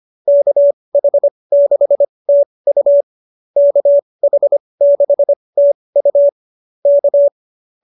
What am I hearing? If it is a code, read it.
Morse code, heard as a keyed tone, about 25 wpm, spaced as for 20 wpm, 590 Hz. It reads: KH6TU KH6TU K
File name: QSO309 20